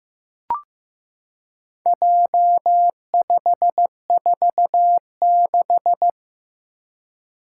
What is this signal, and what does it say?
Morse code, heard as a keyed tone, about 15 wpm, 700 Hz. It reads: J546
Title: Four Challenging Letter-Number Combinations 15wpm